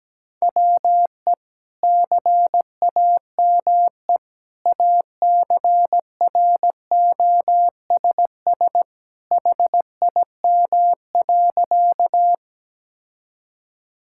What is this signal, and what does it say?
Morse code, heard as a keyed tone, about 17 wpm, 700 Hz. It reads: WE CAME ACROSS HIM.
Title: Sentences from Top 1000 Words - Review Part 5 - 17wpm